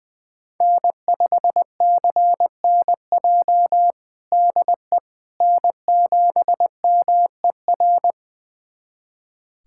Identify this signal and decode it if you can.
Morse code, heard as a keyed tone, about 19 wpm, 700 Hz. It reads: N5CNJ DE N7MER
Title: USA Callsign DE Callsign - 5 Characters - 20wpm